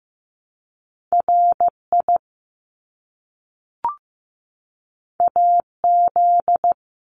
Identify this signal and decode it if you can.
Morse code, heard as a keyed tone, about 15 wpm, 700 Hz. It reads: RI AZ
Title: ARRL Sections 15wpm